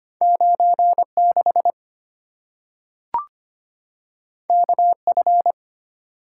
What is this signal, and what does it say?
Morse code, heard as a keyed tone, about 25 wpm, 700 Hz. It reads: 96 KF
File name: Two Letter-Number Combinations - 25wpm